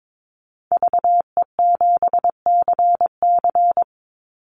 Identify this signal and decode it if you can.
Morse code, heard as a keyed tone, about 22 wpm, 700 Hz. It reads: VE7CC
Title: Call Signs - 22wpm